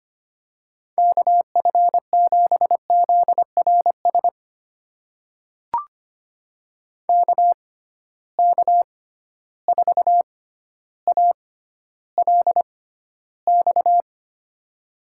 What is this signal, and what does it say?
Morse code, heard as a keyed tone, about 25 wpm, spaced as for 13 wpm, 700 Hz. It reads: KF7ZRS KK4ALX
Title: USA Call Signs - 6 Characters - 6x Character Spacing - 25wpm